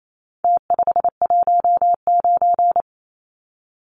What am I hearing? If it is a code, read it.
Morse code, heard as a keyed tone, about 28 wpm, 700 Hz. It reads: T519
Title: Numbers - Contest Style - 4-Digit 28wpm